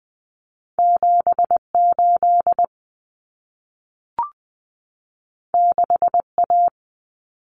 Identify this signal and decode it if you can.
Morse code, heard as a keyed tone, about 20 wpm, 700 Hz. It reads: 78 6A
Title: Common Cut Numbers - 2-Digit 20wpm